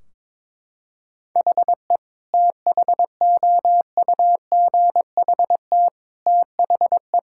Morse code, heard as a keyed tone, about 22 wpm, 700 Hz. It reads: HE THOUGHT THE
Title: Sentences from Top 1600 Words 22wpm